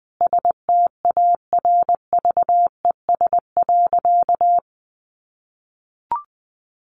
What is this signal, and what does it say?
Morse code, heard as a keyed tone, about 20 wpm, 700 Hz. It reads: STARVES.
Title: A Sound Of Thunder 20wpm